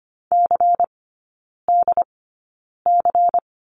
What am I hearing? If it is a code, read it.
Morse code, heard as a keyed tone, about 25 wpm, spaced as for 9 wpm, 700 Hz. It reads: CDC
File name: course / 25:10wpm / 093 Lesson 16e - Letter c - Three Characters 25wpm